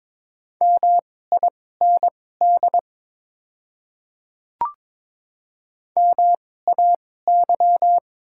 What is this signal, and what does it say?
Morse code, heard as a keyed tone, about 22 wpm, spaced as for 16 wpm, 700 Hz. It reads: MIND MAY
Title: 2x Character Spacing - Top 500 Words - 22wpm